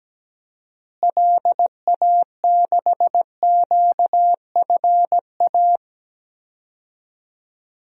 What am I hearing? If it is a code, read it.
Morse code, heard as a keyed tone, about 17 wpm, 700 Hz. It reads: LA6QFA
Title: DXCC-100 Call Signs - 6 Characters - Repeated 2x - 17wpm